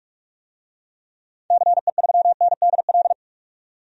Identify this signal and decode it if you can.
Morse code, heard as a keyed tone, about 45 wpm, 700 Hz. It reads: CE3NDL
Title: DXCC-100 Call Signs - 6 Characters - 45wpm